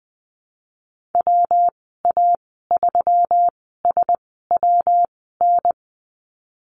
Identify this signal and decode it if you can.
Morse code, heard as a keyed tone, about 20 wpm, spaced as for 15 wpm, 700 Hz. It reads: WA3SWN
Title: 179 Lesson 29g - Slash - Call Signs 20wpm